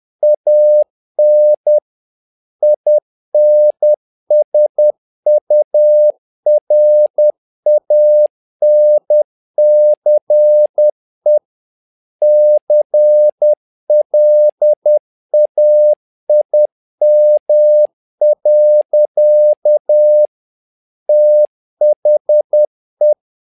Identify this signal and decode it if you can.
Morse code, heard as a keyed tone, about 10 wpm, 600 Hz. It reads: AN INSURANCE CLAIM. THE
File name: learning center / tower trouble / tower trouble 10wpm